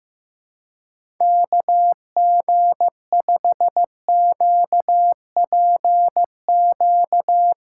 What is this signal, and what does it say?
Morse code, heard as a keyed tone, about 15 wpm, 700 Hz. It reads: KG5QPQ
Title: USA Callsign DE Callsign - 6 Characters - 15wpm